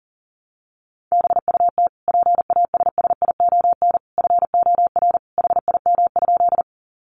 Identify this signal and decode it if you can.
Morse code, heard as a keyed tone, about 40 wpm, 700 Hz. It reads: BUT PASSION FOR HIM?
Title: Sentences from Top 1300 Words 40wpm